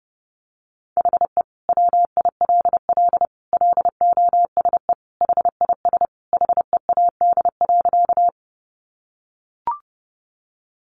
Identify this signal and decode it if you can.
Morse code, heard as a keyed tone, about 30 wpm, 700 Hz. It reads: HE WILL LOSE HIS HEAD.
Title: Sentences from Top 1100 Words 30wpm